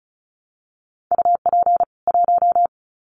Morse code, heard as a keyed tone, about 34 wpm, 700 Hz. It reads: UP 1
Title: Common QSO Phrases 35wpm